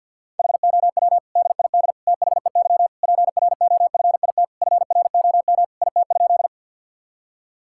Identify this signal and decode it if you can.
Morse code, heard as a keyed tone, about 49 wpm, 690 Hz. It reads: HOW DID THEY PROFIT FROM IT?
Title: Sentences from Top 1000 Words - Review Part 5 - 50wpm